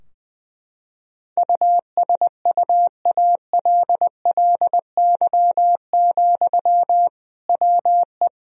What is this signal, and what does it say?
Morse code, heard as a keyed tone, about 20 wpm, 700 Hz. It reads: USUALLY, WE
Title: Sentences from Top 1000 Words - Review Part 3 - 20wpm